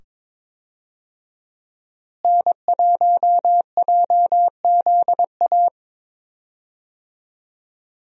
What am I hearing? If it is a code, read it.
Morse code, heard as a keyed tone, about 21 wpm, 700 Hz. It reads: N1JZA
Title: USA Call Signs - 5 Characters - Repeated 2x - 22wpm